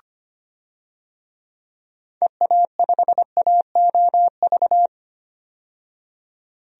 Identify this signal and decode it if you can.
Morse code, heard as a keyed tone, about 25 wpm, 700 Hz. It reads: EA5AOV
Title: DXCC-100 Call Signs - 6 Characters - 25wpm